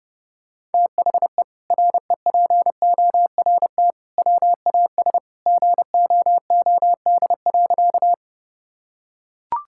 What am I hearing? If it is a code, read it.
Morse code, heard as a keyed tone, about 30 wpm, 700 Hz. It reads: THE REPORT WAS GOOD.
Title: Sentences from Top 1000 Words - Review Part 6 - 30wpm